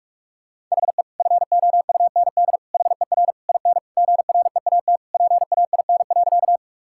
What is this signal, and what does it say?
Morse code, heard as a keyed tone, about 41 wpm, 700 Hz. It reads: HE FOUND HER IN GREAT PAIN.
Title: Sentences from Top 1600 Words 45wpm